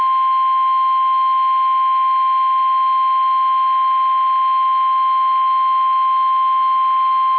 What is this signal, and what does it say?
Emergency Alert System (SAME) tones (the 1050 Hz attention tone). It is heard from 0.5 to 6.7 s.
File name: eas / KUSO / KUSO